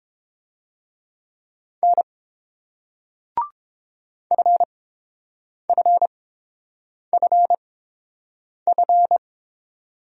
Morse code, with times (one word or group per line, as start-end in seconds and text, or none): 1.8-2.0 N
4.3-9.2 FFFF